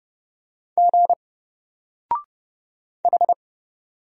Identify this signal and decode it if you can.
Morse code, heard as a keyed tone, about 30 wpm, 700 Hz. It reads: G H
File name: Single-Letter-Number-30wpm